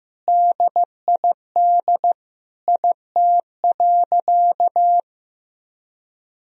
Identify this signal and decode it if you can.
Morse code, heard as a keyed tone, about 15 wpm, 700 Hz. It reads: DID IT.
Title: Sentences from Top 1000 Words - Review Part 4 - 15wpm